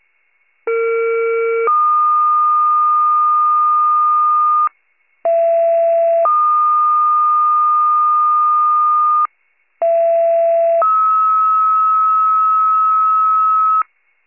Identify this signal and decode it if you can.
Two-tone page, heard as a one-second tone, then a three-second tone, about 460 then 1220 Hz, about 670 then 1220 Hz, about 670 then 1290 Hz. It is heard from 0.7 to 13.8 s.
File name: 41 RESCUE DUTY